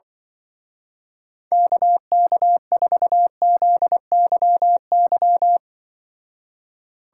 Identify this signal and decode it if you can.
Morse code, heard as a keyed tone, about 24 wpm, 700 Hz. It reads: KK4ZYY
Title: Speed Racing - USA 2x3 Call Signs 20wpm